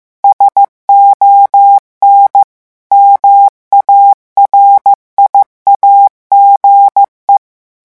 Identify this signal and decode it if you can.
Morse code, heard as a keyed tone, about 15 wpm, 800 Hz. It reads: SON MARIAGE
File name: morse mp3 / revision9 citations humoristiques vitesse 14 mots minute